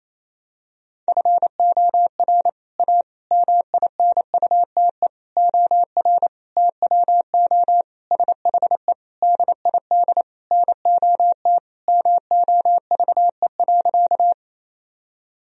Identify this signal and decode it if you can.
Morse code, heard as a keyed tone, about 28 wpm, 700 Hz. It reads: FOR A MINUTE OR TWO SHE DID NOT MOVE.
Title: Sentences from Top 1000 Words - Review Part 1 - 28wpm